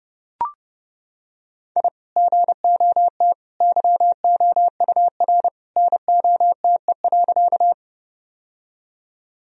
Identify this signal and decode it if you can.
Morse code, heard as a keyed tone, about 32 wpm, 700 Hz. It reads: I GOT YOUR NOTE.
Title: Sentences from Top 1000 Words - Review Part 2 - 30wpm